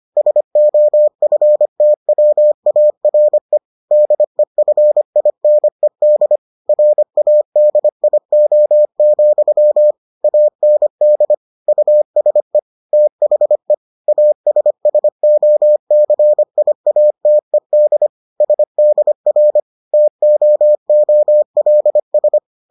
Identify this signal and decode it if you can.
Morse code, heard as a keyed tone, about 25 wpm, 600 Hz. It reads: SOFTWARE DEFINED RADIO, AND USE THE ASSOCIATED SDR TOOLS